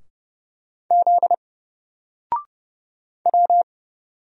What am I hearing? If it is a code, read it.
Morse code, heard as a keyed tone, about 30 wpm, 700 Hz. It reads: Z W
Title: Single Letter-Number 30wpm